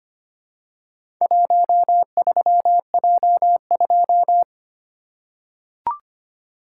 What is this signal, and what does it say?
Morse code, heard as a keyed tone, about 25 wpm, 700 Hz. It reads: 13J2
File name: practice / 25wpm / Four Letter-Number Combinations - 25wpm